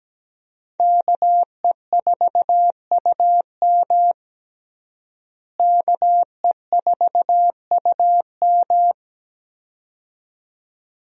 Morse code, heard as a keyed tone, about 17 wpm, 700 Hz. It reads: KE4UM KE4UM
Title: USA Call Signs - 5 Characters - Repeated 2x - 17wpm